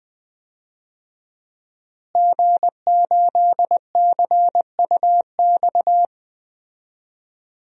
Morse code, heard as a keyed tone, about 20 wpm, 700 Hz. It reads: G8CUX
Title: DXCC-100 Call Signs - 5 Characters - Repeated 2x - 20wpm